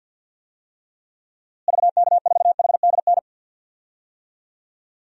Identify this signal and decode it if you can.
Morse code, heard as a keyed tone, about 52 wpm, 700 Hz. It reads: VK4HDN
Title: DXCC-100 Call Signs - 6 Characters - Repeated 3x - 50wpm